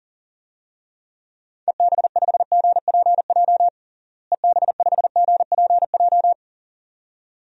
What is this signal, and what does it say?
Morse code, heard as a keyed tone, about 37 wpm, 700 Hz. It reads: EB5GPJ EB5GPJ
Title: DXCC-100 Call Signs - 6 Characters - Repeated 2x - 40wpm